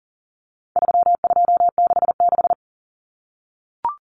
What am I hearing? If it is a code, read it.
Morse code, heard as a keyed tone, about 40 wpm, 700 Hz. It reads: 3266
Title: Numbers - 4-Digit 40wpm